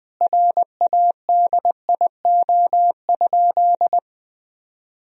Morse code, heard as a keyed tone, about 21 wpm, 700 Hz. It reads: RADIO?
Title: Sentences from Top 1000 Words - Review Part 3 - 20wpm